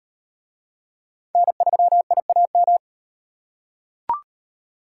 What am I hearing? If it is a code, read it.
Morse code, heard as a keyed tone, about 36 wpm, 700 Hz. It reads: N3IAM